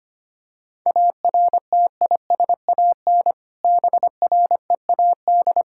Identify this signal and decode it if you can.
Morse code, heard as a keyed tone, about 25 wpm, 700 Hz. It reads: ARTISAN BREAD
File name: practice / 25wpm / Difficult Words with Hints - 25wpm